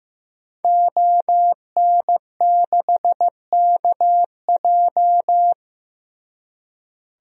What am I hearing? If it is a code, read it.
Morse code, heard as a keyed tone, about 15 wpm, 700 Hz. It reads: ON6KJ
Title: DXCC-100 Call Signs - 5 Characters - Repeated 2x - 15wpm